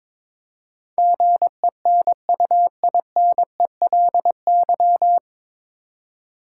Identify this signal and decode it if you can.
Morse code, heard as a keyed tone, about 22 wpm, 700 Hz. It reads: GENUINELY
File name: 9 Letter Words - First 3 Letters Given - 22wpm